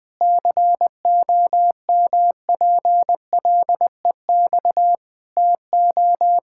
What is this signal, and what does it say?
Morse code, heard as a keyed tone, about 20 wpm, 700 Hz. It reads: COMPLEX TO
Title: Sentences from Top 1000 Words - Review Part 6 - 20wpm